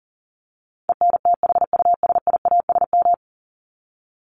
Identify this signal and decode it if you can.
Morse code, heard as a keyed tone, about 40 wpm, 700 Hz. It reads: ENTHUSIASM
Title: Top 1751-2000 Words 40wpm